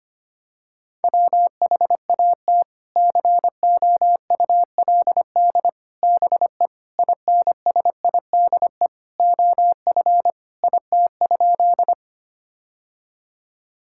Morse code, heard as a keyed tone, about 25 wpm, 700 Hz. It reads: WHAT COULD BE INSIDE OF IT?
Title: Sentences from Top 1000 Words - Review Part 6 - 25wpm